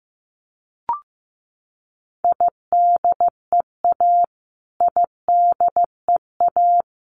Morse code, heard as a keyed tone, about 15 wpm, 700 Hz. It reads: IDEA IDEA
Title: Top 300 Words Repeated 15wpm